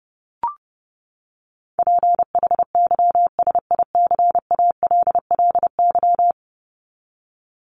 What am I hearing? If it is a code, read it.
Morse code, heard as a keyed tone, about 30 wpm, 700 Hz. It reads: PHYSICALLY
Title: Top 1501-1750 Words 30wpm